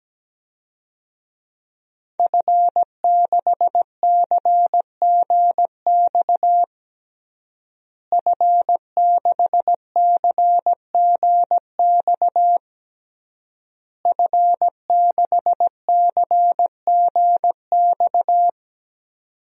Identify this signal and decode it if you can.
Morse code, heard as a keyed tone, about 17 wpm, 700 Hz. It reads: F6CGX F6CGX F6CGX
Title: DXCC-100 Call Signs - 5 Characters - Repeated 3x - 17wpm